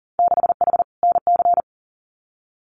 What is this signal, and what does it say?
Morse code, heard as a keyed tone, about 40 wpm, 700 Hz. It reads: 6H NC
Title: Winter Field Day Exchanges Encore 40wpm